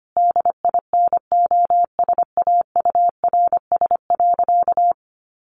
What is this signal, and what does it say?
Morse code, heard as a keyed tone, about 25 wpm, 700 Hz. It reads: DINOSAURS.
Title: A Sound Of Thunder 25wpm